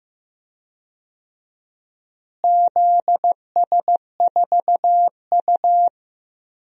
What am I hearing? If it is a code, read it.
Morse code, heard as a keyed tone, about 16 wpm, 700 Hz. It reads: ZS4U
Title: DXCC-100 Call Signs - 4 Characters - Repeated 2x - 15wpm